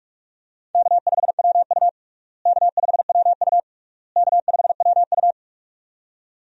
Morse code, heard as a keyed tone, about 46 wpm, 700 Hz. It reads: K5WU K5WU K5WU
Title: USA Call Signs - 4 Characters - Repeated 3x - 45wpm